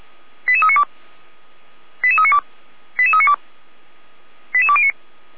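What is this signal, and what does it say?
Five-tone selective call (ZVEI), heard as quick runs of five tones, tones of about 70 ms. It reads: 88392 88392 88392 88209
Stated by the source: ZVEI five-tone selective call, address 88228, received on an SDR.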